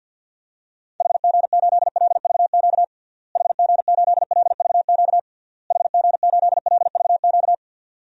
Morse code, heard as a keyed tone, about 51 wpm, 700 Hz. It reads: HG8LVQ HG8LVQ HG8LVQ
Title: DXCC-100 Call Signs - 6 Characters - Repeated 3x - 50wpm